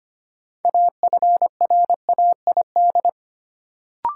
Morse code, heard as a keyed tone, about 25 wpm, 700 Hz. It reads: AFRAID
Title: Sentences from Top 1500 Words - New Word Drill Alt - 25wpm